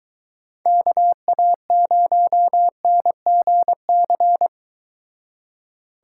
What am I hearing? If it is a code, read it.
Morse code, heard as a keyed tone, about 23 wpm, 700 Hz. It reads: KA0NGC